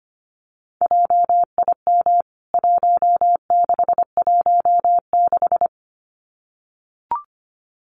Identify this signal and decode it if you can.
Morse code, heard as a keyed tone, about 25 wpm, 700 Hz. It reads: JIM 1616
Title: CWT Exchange 25wpm